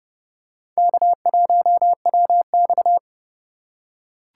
Morse code, heard as a keyed tone, about 28 wpm, 700 Hz. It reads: K1WX